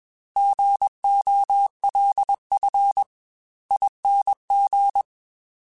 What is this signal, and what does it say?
Morse code, heard as a keyed tone, about 21 wpm, spaced as for 18 wpm, 800 Hz. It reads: GOLFING
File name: Alphabetical / Alphabetical mp3s / Gs @20